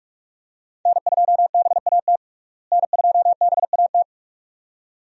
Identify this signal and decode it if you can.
Morse code, heard as a keyed tone, about 46 wpm, 700 Hz. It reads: N2BAT N2BAT
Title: USA Call Signs - 5 Characters - Repeated 2x - 45wpm